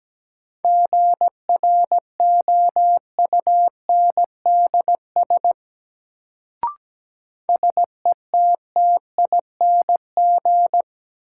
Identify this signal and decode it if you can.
Morse code, heard as a keyed tone, about 17 wpm, 700 Hz. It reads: GROUNDS SETTING
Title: Sentences from Top 1300 Words - New Word Drill Alt - 17wpm